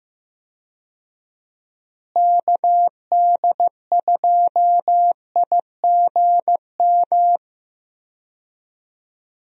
Morse code, heard as a keyed tone, about 15 wpm, 700 Hz. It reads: KD2IGM
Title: USA Call Signs - 6 Characters - Repeated 2x - 15wpm